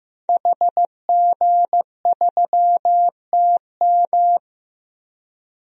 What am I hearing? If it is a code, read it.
Morse code, heard as a keyed tone, about 15 wpm, 700 Hz. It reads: HG3TM
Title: DXCC-100 Call Signs - 5 Characters - Repeated 2x - 15wpm